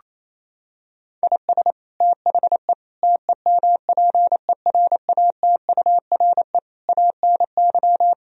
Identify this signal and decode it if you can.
Morse code, heard as a keyed tone, about 28 wpm, 700 Hz. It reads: IS THE TEMPERATURE ANY
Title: Sentences from Top 1000 Words - Review Part 4 - 28wpm